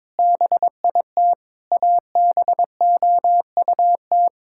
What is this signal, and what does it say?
Morse code, heard as a keyed tone, about 21 wpm, 700 Hz. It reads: BIT ABOUT
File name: practice / 22wpm / Sets of 8 Words - 22wpm